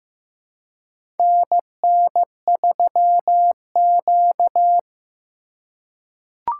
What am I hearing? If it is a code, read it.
Morse code, heard as a keyed tone, about 15 wpm, 700 Hz. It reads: NN3Q
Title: USA Call Signs - 4 Characters - 3x Character Spacing - 15wpm